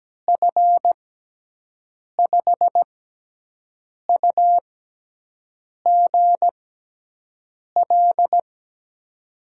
Morse code, heard as a keyed tone, about 17 wpm, spaced as for 6 wpm, 700 Hz. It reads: F5UGL
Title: USA Call Signs - 6 Characters - 6x Character Spacing - 17wpm